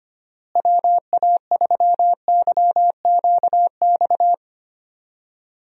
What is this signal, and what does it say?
Morse code, heard as a keyed tone, about 25 wpm, 700 Hz. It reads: WA3YQX
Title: USA Call Signs - 6 Characters - Repeated 2x - 25wpm